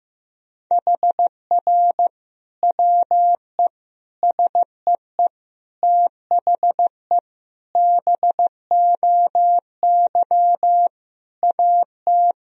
Morse code, heard as a keyed tone, about 15 wpm, 700 Hz. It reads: HR WE SEE THE BOY AT
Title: Sentences from Top 300 Words and Abbreviations 15wpm